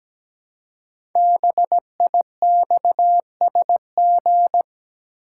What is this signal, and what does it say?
Morse code, heard as a keyed tone, about 17 wpm, 700 Hz. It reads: BIXSG
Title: Code Groups - 5 Letters - 17wpm